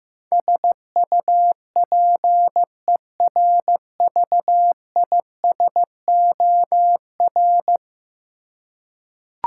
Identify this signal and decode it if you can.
Morse code, heard as a keyed tone, about 15 wpm, 700 Hz. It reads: SUPERVISOR
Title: Practice QSO - Exchange 3 - More Info - 15wpm